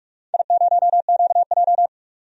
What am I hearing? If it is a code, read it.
Morse code, heard as a keyed tone, about 42 wpm, 700 Hz. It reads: I0QJ
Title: DXCC-100 Call Signs - 4 Characters - 45wpm